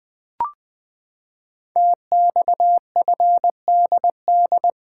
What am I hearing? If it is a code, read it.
Morse code, heard as a keyed tone, about 20 wpm, 700 Hz. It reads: TXFDD
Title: Code Groups - 5 Letters - 20wpm